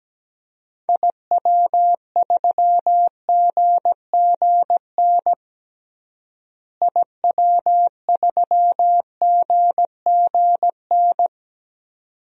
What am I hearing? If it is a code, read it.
Morse code, heard as a keyed tone, about 17 wpm, 700 Hz. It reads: IW3GGN IW3GGN
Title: DXCC-100 Call Signs - 6 Characters - Repeated 2x - 17wpm